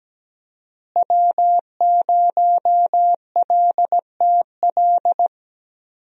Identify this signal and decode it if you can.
Morse code, heard as a keyed tone, about 17 wpm, 700 Hz. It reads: W0LTL
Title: USA Call Signs - 5 Characters - Repeated 3x - 17wpm